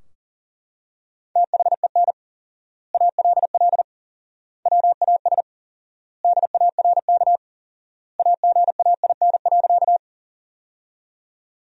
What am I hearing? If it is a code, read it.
Morse code, heard as a keyed tone, about 40 wpm, 700 Hz. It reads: THEN ALL WAS DARK AGAIN.
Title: 4x Word-Spacing - Sentences from Top 500 Words - 40wpm